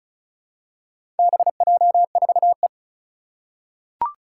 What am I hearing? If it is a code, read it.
Morse code, heard as a keyed tone, about 36 wpm, 700 Hz. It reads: BJ4E
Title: Four Challenging Letter-Number Combinations 35wpm